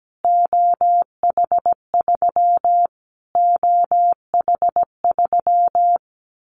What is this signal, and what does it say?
Morse code, heard as a keyed tone, about 17 wpm, 700 Hz. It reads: OH3 OH3
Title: Callsign Prefixes for Top 25 DXCC - Repeated 3x - 17wpm